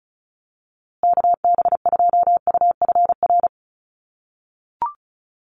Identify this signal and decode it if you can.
Morse code, heard as a keyed tone, about 34 wpm, 700 Hz. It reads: KB2UFR
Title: USA Call Signs Repeated 3x 35wpm